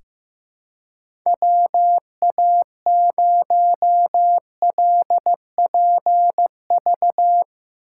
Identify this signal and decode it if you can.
Morse code, heard as a keyed tone, about 15 wpm, 700 Hz. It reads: WA0LPV
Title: K1USN SST QSO Practice - 15wpm